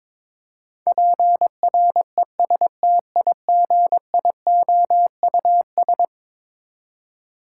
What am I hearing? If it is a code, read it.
Morse code, heard as a keyed tone, about 22 wpm, 700 Hz. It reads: PRESTIGIOUS
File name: practice / 22wpm / Eleven Letter Words - 22wpm